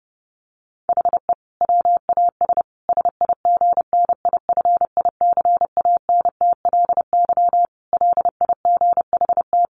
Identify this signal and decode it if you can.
Morse code, heard as a keyed tone, about 30 wpm, 700 Hz. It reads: HE WAS SIGNIFICANTLY LIGHT
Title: Sentences from Top 1500 Words 30wpm